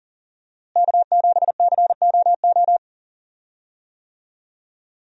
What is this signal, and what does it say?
Morse code, heard as a keyed tone, about 40 wpm, 700 Hz. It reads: K7COO
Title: USA Call Signs - 5 Characters - Repeated 3x - 40wpm